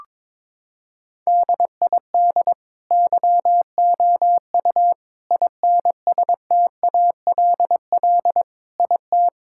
Morse code, heard as a keyed tone, about 22 wpm, 700 Hz. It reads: DID YOU INSTALL IT
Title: Sentences from Top 1600 Words 22wpm